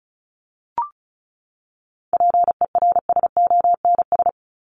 Morse code, heard as a keyed tone, about 34 wpm, 700 Hz. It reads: PERSONS
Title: ES:S Suffix - Top 100 Words with Base Words 35wpm